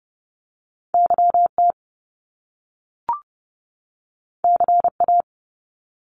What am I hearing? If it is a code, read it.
Morse code, heard as a keyed tone, about 30 wpm, 700 Hz. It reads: YT CA
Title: Top Two Letter Combinations in Words 30wpm